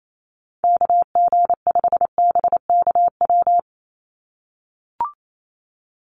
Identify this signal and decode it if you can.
Morse code, heard as a keyed tone, about 28 wpm, 700 Hz. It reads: KG5BKW
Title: USA Call Signs 28wpm